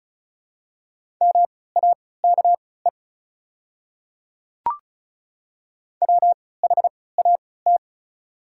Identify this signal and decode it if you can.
Morse code, heard as a keyed tone, about 36 wpm, spaced as for 20 wpm, 690 Hz. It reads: MAKE WHAT
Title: 3x Character Spacing - Top 100 Words - 35wpm